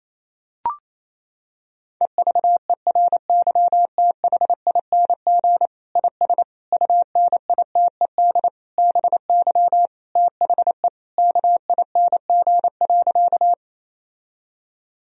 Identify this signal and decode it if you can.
Morse code, heard as a keyed tone, about 28 wpm, 700 Hz. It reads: EVERYTHING IS UNITED BY THE KING.
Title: Sentences from Top 1000 Words - Review Part 4 - 28wpm